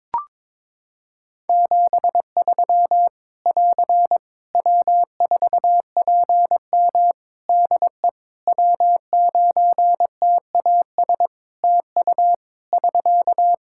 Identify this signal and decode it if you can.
Morse code, heard as a keyed tone, about 22 wpm, 700 Hz. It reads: 73 <AR> W4PM DE W9TAS TU <SK>
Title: K1USN SST QSO Practice 22wpm